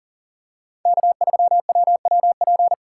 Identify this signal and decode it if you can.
Morse code, heard as a keyed tone, about 40 wpm, 700 Hz. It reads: K3WWP
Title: K1USN SST QSO Practice - 40wpm